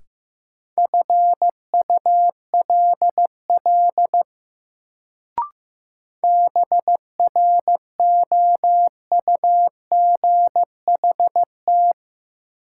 Morse code, heard as a keyed tone, about 15 wpm, 700 Hz. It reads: FULL BROUGHT
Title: Sentences from Top 500 Words - New Word Drill - 15wpm